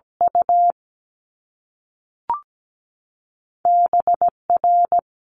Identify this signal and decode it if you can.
Morse code, heard as a keyed tone, about 17 wpm, 700 Hz. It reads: U B R
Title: Top Two Letter Combinations in Words 17wpm